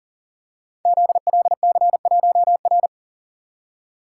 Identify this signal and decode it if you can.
Morse code, heard as a keyed tone, about 43 wpm, 700 Hz. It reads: ZLC1R
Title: Code Groups - 5 Letter:Numbers - 40wpm